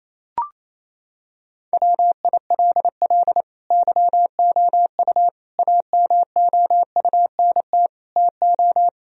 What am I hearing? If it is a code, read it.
Morse code, heard as a keyed tone, about 28 wpm, 700 Hz. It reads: WILL YOU AMOUNT TO
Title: Sentences from Top 1000 Words - Review Part 4 - 28wpm